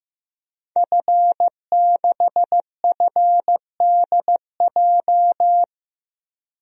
Morse code, heard as a keyed tone, about 15 wpm, 700 Hz. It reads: F6FDJ
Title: DXCC-100 Call Signs - 5 Characters - 15wpm